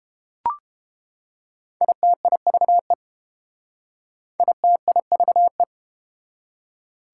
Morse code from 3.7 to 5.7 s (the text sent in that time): ITIVE